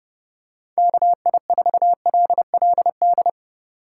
Morse code, heard as a keyed tone, about 32 wpm, 700 Hz. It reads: KI4LLD
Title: USA Call Signs - 6 Characters - 3x Character Spacing - 30wpm